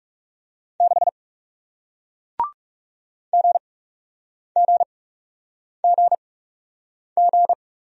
Morse code, heard as a keyed tone, about 40 wpm, spaced as for 8 wpm, 700 Hz. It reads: 6 GGGG